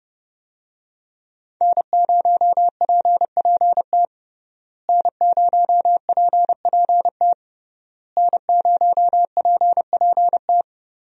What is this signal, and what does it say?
Morse code, heard as a keyed tone, about 30 wpm, 700 Hz. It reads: N0PPT N0PPT N0PPT
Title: USA Call Signs - 5 Characters - Repeated 3x - 30wpm